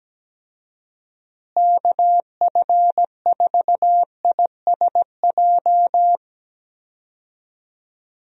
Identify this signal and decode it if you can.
Morse code, heard as a keyed tone, about 17 wpm, 700 Hz. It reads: KF4ISJ
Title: USA Call Signs - 6 Characters - Repeated 3x - 17wpm